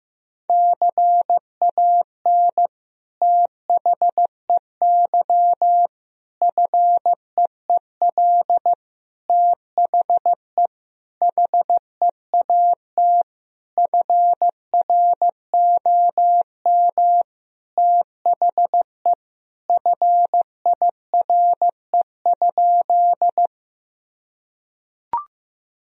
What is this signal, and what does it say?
Morse code, heard as a keyed tone, about 15 wpm, 700 Hz. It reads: CAN THEY FEEL THE HEAT FROM THE FIRE?
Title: Sentences from Top 1000 Words - Review Part 3 - 15wpm